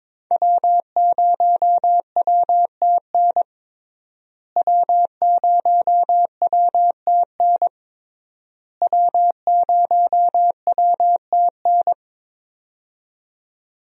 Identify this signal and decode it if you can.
Morse code, heard as a keyed tone, about 21 wpm, 700 Hz. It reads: W0WTN W0WTN W0WTN
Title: USA Call Signs - 5 Characters - Repeated 3x - 22wpm